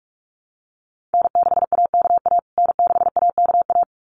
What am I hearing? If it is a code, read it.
Morse code, heard as a keyed tone, about 44 wpm, 700 Hz. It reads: N6AKA N6AKA
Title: USA Call Signs Repeated 2x 45wpm